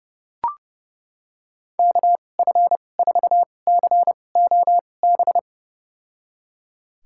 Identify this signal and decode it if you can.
Morse code, heard as a keyed tone, about 32 wpm, spaced as for 22 wpm, 700 Hz. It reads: KF4COB
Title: USA Call Signs - 6 Characters - 2x Character Spacing - 30wpm